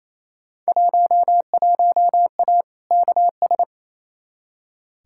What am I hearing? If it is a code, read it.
Morse code, heard as a keyed tone, about 28 wpm, 700 Hz. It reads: 11A KS
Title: Field Day Exchanges Encore - 28wpm